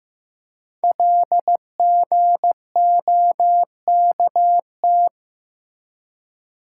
Morse code, heard as a keyed tone, about 15 wpm, 700 Hz. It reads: LGOKT
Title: Code Groups - 5 Letters - 15wpm